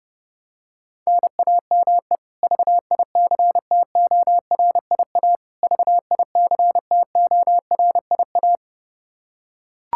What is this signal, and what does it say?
Morse code, heard as a keyed tone, about 30 wpm, 700 Hz. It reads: NAME VICTORIA VICTORIA
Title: K1USN SST QSO Practice 30wpm